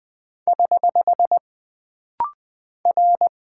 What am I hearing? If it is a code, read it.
Morse code, heard as a keyed tone, about 20 wpm, 700 Hz. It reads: <HH> R
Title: Single Character (Letter:Number:Punctuation:Prosign) - Rapid-Fire - 20wpm